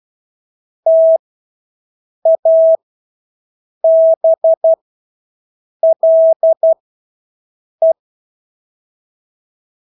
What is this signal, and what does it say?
Morse code, heard as a keyed tone, about 12 wpm, spaced as for 6 wpm, 650 Hz. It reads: TABLE
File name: Engelske ord 12-6 wpm